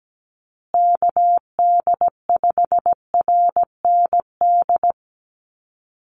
Morse code, heard as a keyed tone, about 17 wpm, 700 Hz. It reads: KD5RND
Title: USA Call Signs Repeated 2x 17wpm